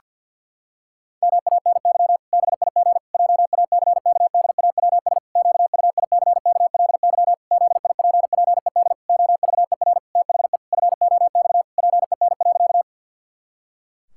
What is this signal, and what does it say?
Morse code, heard as a keyed tone, about 50 wpm, 700 Hz. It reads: MANY BIG JACKDAWS QUICKLY ZIPPED OVER THE FOX PEN.
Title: Pangram Challenge v2 - 50wpm